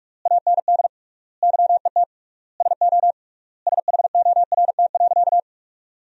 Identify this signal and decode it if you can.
Morse code, heard as a keyed tone, about 42 wpm, 700 Hz. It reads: AND YET SO SHORT.
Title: 3x Word-Spacing - Sentences from Top 500 Words - 45wpm